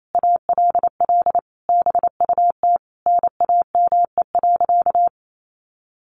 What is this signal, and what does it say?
Morse code, heard as a keyed tone, about 28 wpm, 700 Hz. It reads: ALL BUT NAME.
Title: Sentences from Top 1500 Words 28wpm